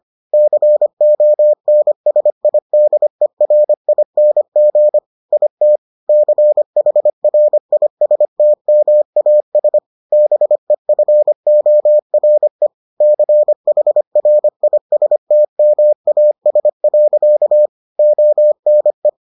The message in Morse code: CONSIDERING IT CHRISTMAS BEFORE CHRISTMAS. ONE

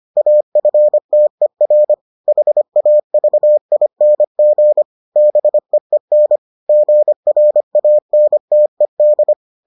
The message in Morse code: AFTER HAVING BEEN GRANTED

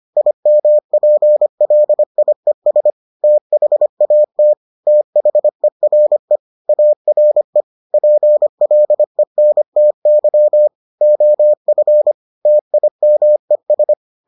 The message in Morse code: IMPLIES THAT THERE ARE PLENTY OF TIMES

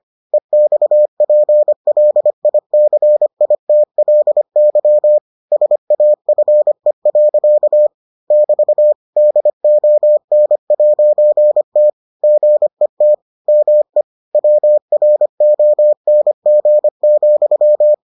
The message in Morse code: EXPLICITLY SAFE. <BT> DON'T GET ME WRONG,

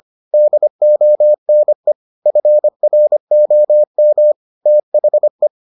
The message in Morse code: DONE FROM THE

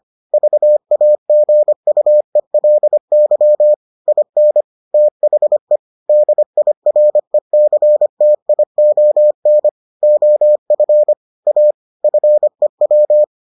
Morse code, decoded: VAGUELY IN THE DIRECTION OF A FEW

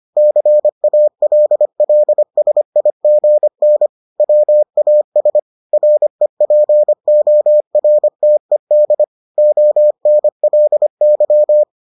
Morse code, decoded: CALLSIGN WAS REPORTED ONLY

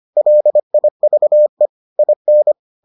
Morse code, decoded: LIVE IN

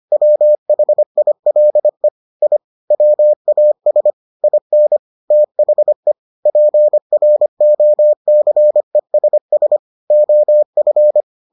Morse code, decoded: WHILE I WAS IN THE PROCESS OF

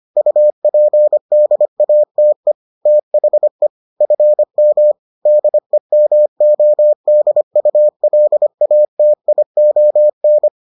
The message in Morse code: UPDATE THE FM DEMODULATION

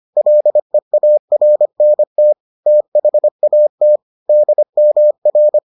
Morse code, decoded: LEARNT THAT DMR

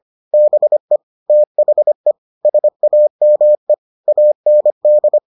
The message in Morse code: BE THE SAME AND

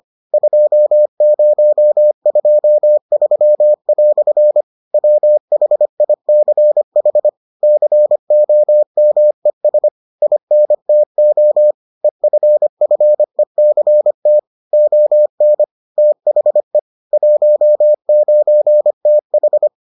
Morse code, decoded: 2023" WHICH COMES INTO EFFECT ON THE 19TH

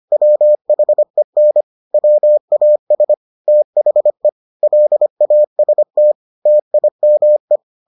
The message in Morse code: WHEN WAS THE LAST TIME

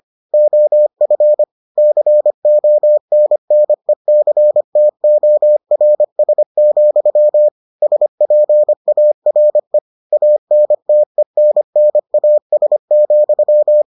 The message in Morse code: OF CONNECTORS, SPARE ANTENNAS,

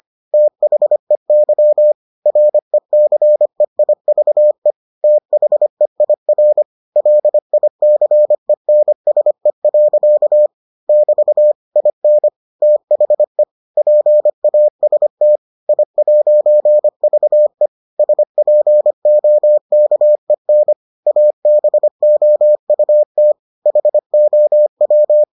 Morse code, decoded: THEY RECEIVE THEIR LICENSE. <BT> IN THE PAST I'VE SPOKEN ABOUT HOW

